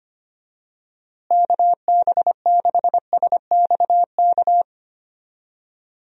Morse code, heard as a keyed tone, about 25 wpm, 700 Hz. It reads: KB6SXK
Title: USA Call Signs - 6 Characters - Repeated 2x - 25wpm